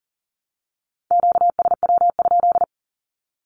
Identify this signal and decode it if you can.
Morse code, heard as a keyed tone, about 40 wpm, 700 Hz. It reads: QSW?
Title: Q-Codes Repeated 3x 40wpm